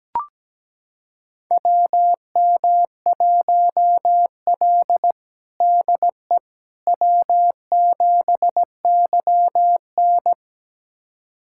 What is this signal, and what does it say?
Morse code, heard as a keyed tone, about 17 wpm, 700 Hz. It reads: WM1L DE W7YN
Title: USA Callsign DE Callsign - 4 Characters - 17wpm